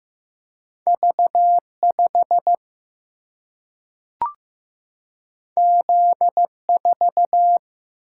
Morse code, heard as a keyed tone, about 15 wpm, 700 Hz. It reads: V 5 Z 4
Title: Single-Letter Single-Number 15wpm